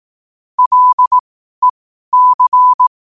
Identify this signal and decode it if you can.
Morse code, heard as a keyed tone, about 18 wpm, spaced as for 12 wpm, 1000 Hz. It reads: LEC
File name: qotdfiles / 12wpm